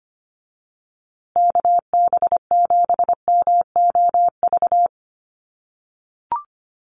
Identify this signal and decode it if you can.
Morse code, heard as a keyed tone, about 25 wpm, 700 Hz. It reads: KB7MOV
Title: USA Call Signs Repeated 3x 25wpm